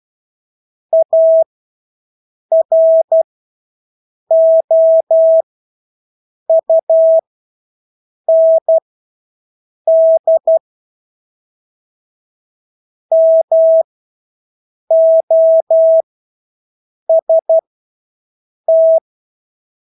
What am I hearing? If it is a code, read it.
Morse code, heard as a keyed tone, about 12 wpm, spaced as for 6 wpm, 650 Hz. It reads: AROUND MOST
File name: Engelske ord 12-6 wpm